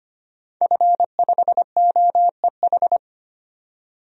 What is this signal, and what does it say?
Morse code, heard as a keyed tone, about 25 wpm, 700 Hz. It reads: F5OEH
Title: DXCC-100 Call Signs - 5 Characters - 25wpm